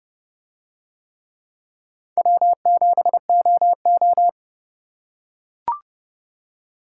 Morse code, heard as a keyed tone, about 29 wpm, 700 Hz. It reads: W7OO